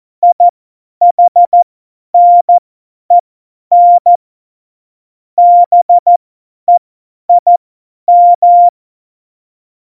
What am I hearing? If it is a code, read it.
Morse code, heard as a keyed tone, about 14 wpm, spaced as for 10 wpm, 700 Hz. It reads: IHNEN BEIM
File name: Morsekurs / Witze / Witz 08. Dezember 2020, 50 F70